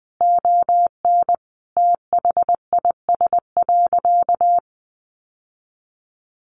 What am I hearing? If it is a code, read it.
Morse code, heard as a keyed tone, about 20 wpm, 700 Hz. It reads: ON THIS.
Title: Sentences from Top 1200 Words 20wpm